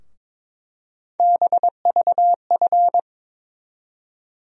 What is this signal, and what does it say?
Morse code, heard as a keyed tone, about 22 wpm, 700 Hz. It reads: BVF
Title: USA Call Sign Postfixes - 3 Letters 22wpm